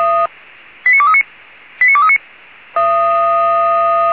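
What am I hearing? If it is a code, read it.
Five-tone selective call (ZVEI), heard as quick runs of five tones, tones of about 70 ms. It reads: 89239 89239
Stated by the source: ZVEI five-tone selective call received on an SDR.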